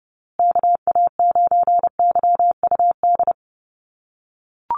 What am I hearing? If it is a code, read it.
Morse code, heard as a keyed tone, about 30 wpm, 700 Hz. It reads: KA9YUD
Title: USA Call Signs Repeated 3x 30wpm